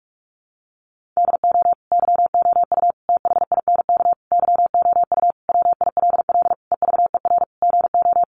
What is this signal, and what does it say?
Morse code, heard as a keyed tone, about 44 wpm, 700 Hz. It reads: DO YOU THINK YOU WILL EVER GO
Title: Sentences from Top 1300 Words 45wpm